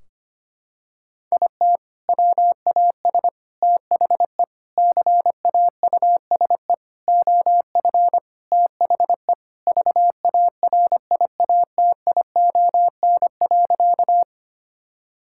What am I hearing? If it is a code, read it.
Morse code, heard as a keyed tone, about 25 wpm, 700 Hz. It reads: IT WAS THE CAUSE OF THE VARIATION.
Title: Sentences from Top 1600 Words 25wpm